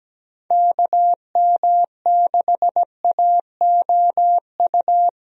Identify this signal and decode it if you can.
Morse code, heard as a keyed tone, about 17 wpm, 700 Hz. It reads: KM6AOU
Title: USA Call Signs - 6 Characters - Repeated 3x - 17wpm